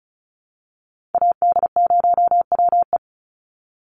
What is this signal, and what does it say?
Morse code, heard as a keyed tone, about 34 wpm, 700 Hz. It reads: AD0WE
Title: Call Signs - 35wpm